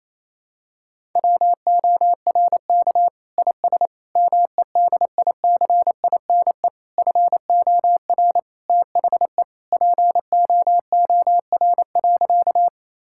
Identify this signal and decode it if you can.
Morse code, heard as a keyed tone, about 28 wpm, 700 Hz. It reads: WORK IS MEDICINE FOR THE POOR.
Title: Sentences from Top 1000 Words - Review Part 6 - 28wpm